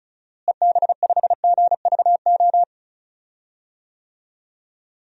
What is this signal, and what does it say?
Morse code, heard as a keyed tone, about 36 wpm, 700 Hz. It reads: EB5GVO
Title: DXCC-100 Call Signs - 6 Characters - Repeated 3x - 35wpm